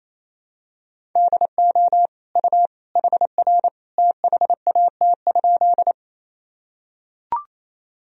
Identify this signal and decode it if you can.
Morse code, heard as a keyed tone, about 28 wpm, 700 Hz. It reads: DO U HR THAT?
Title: Sentences from Top 300 Words and Abbreviations 28wpm